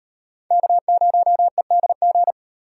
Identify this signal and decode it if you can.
Morse code, heard as a keyed tone, about 38 wpm, 700 Hz. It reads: K0EDG